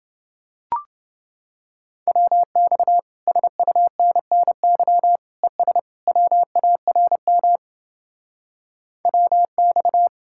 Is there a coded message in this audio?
Yes: Morse code from 2.1 to 10.1 s, heard as a keyed tone, about 31 wpm, 700 Hz. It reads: WX SUNNY ES WARM WX